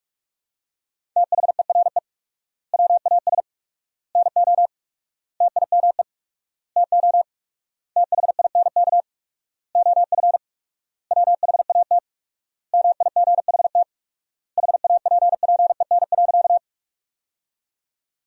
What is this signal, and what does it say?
Morse code, heard as a keyed tone, about 46 wpm, 700 Hz. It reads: THERE WAS NO TIME TO THINK OF WHAT MIGHT HAPPEN.
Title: 4x Word-Spacing - Sentences from Top 400 Words - 45wpm